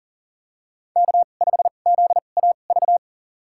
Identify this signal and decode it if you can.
Morse code, heard as a keyed tone, about 40 wpm, spaced as for 29 wpm, 700 Hz. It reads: K5ZAV
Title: USA Call Signs - 5 Characters - 2x Character Spacing - 40wpm